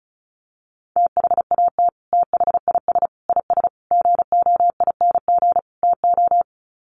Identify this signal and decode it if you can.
Morse code, heard as a keyed tone, about 34 wpm, 700 Hz. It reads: THAT THIS IS GOING TO
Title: Sets of 5 Words Encore2 35wpm